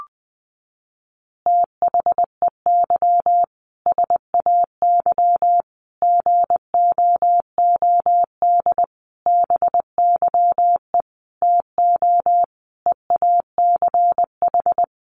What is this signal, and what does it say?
Morse code, heard as a keyed tone, about 20 wpm, 700 Hz. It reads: THEY SAY GOOD BYE TO EACH
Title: Easy Phrases 20wpm